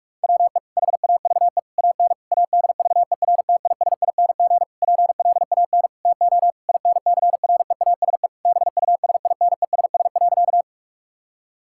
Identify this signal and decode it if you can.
Morse code, heard as a keyed tone, about 46 wpm, 700 Hz. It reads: WE HAVE AN ADVERTISING PLAN TO INCREASE BUSINESS.
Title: Sentences from Top 1000 Words - Review Part 6 - 45wpm